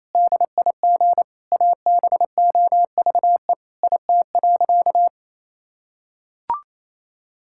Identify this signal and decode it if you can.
Morse code, heard as a keyed tone, about 28 wpm, 700 Hz. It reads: DIG ABOVE IT.
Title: Sentences from Top 1600 Words 28wpm